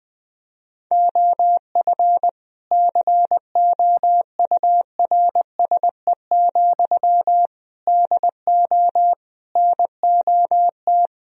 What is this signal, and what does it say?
Morse code, heard as a keyed tone, about 21 wpm, 700 Hz. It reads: OF COURSE, DO NOT
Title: Sentences from Top 1000 Words - Review Part 3 - 20wpm